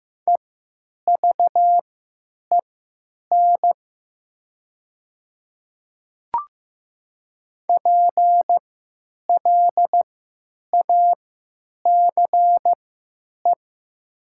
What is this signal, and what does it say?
Morse code, heard as a keyed tone, about 15 wpm, spaced as for 9 wpm, 700 Hz. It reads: EVEN PLACE
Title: 3x Character Spacing - Top 300 Words - 15wpm